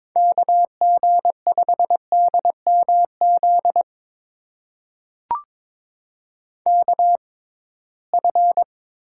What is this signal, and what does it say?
Morse code, heard as a keyed tone, about 22 wpm, spaced as for 17 wpm, 700 Hz. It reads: KG5DMZ KF
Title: USA Call Signs - 6 Characters - 6x Character Spacing - 22wpm